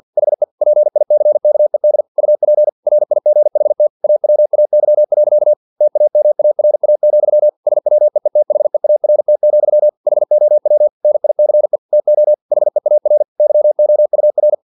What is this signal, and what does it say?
Morse code, heard as a keyed tone, about 50 wpm, 600 Hz. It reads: HE PICKED UP RIGHT AWAY. TAMARA, SWEETHEART, HOW NICE TO HEAR YOUR
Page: Never 49wpm